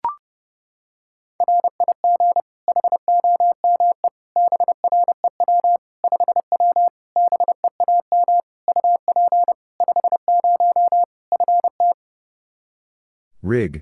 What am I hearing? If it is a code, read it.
Morse code, heard as a keyed tone, about 30 wpm, 700 Hz. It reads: RIG HOME BREW 5W BEAM UP 50 FT
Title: Practice QSO - Exchange 2 - Info - 30wpm